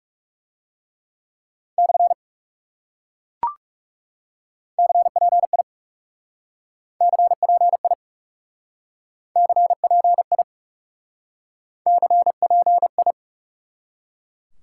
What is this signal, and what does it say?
Morse code, heard as a keyed tone, about 40 wpm, 700 Hz. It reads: / CPI CPI CPI CPI